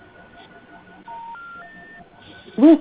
Special information tone, with noise.
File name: recordings / RE a1f8b5e5c47f1de81e1afd0861